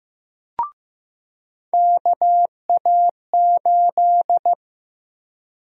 Morse code, heard as a keyed tone, about 15 wpm, 700 Hz. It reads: KA8
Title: USA Call Sign Prefixes - 15wpm